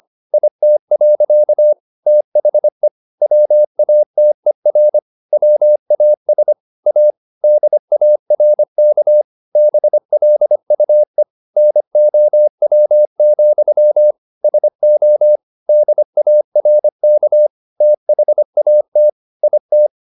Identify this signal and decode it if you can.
Morse code, heard as a keyed tone, about 25 wpm, 600 Hz. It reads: IT. THE WATER WAS A DARK BLUE NOW, SO DARK THAT IT
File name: the old man and the sea part 1 25wpm